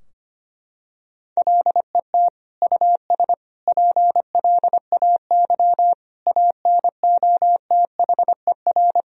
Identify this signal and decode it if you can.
Morse code, heard as a keyed tone, about 25 wpm, 700 Hz. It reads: LET US PLAY ANOTHER
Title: Sentences from Top 1000 Words - Review Part 3 - 25wpm